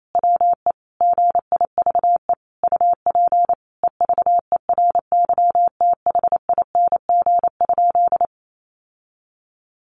Morse code, heard as a keyed tone, about 28 wpm, 700 Hz. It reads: WE GIVE UP EVERYTHING?
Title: Sentences from Top 700 Words 28wpm